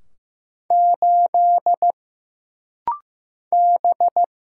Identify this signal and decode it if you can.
Morse code, heard as a keyed tone, about 15 wpm, 700 Hz. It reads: 8 B
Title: Single Letter-Number Alt - Rapid-Fire 15wpm